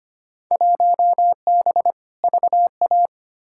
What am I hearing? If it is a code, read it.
Morse code, heard as a keyed tone, about 25 wpm, 700 Hz. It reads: 1B VA
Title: Field Day Exchanges - 25wpm